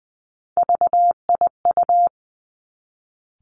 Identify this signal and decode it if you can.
Morse code, heard as a keyed tone, about 19 wpm, 700 Hz. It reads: VIU
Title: Three Challenging Letter-Number Combinations 20wpm